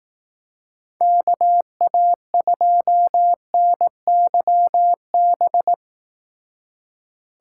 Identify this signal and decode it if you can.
Morse code, heard as a keyed tone, about 18 wpm, 700 Hz. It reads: KA2NYB
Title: Speed Racing - USA 2x3 Call Signs 15wpm